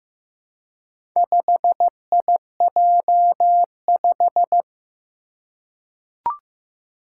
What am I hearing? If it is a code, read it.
Morse code, heard as a keyed tone, about 15 wpm, 700 Hz. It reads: 5IJ5
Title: Four Challenging Letter-Number Combinations 15wpm